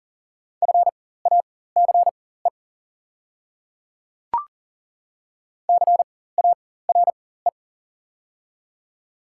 Morse code, heard as a keyed tone, about 40 wpm, spaced as for 19 wpm, 700 Hz. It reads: FACE CARE
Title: 4x Character Spacing - Top 300 Words - 40wpm